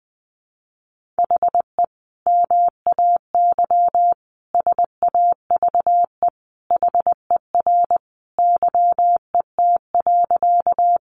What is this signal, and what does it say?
Morse code, heard as a keyed tone, about 20 wpm, 700 Hz. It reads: HE MAY SAVE HER YET.
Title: Sentences from Top 1100 Words 20wpm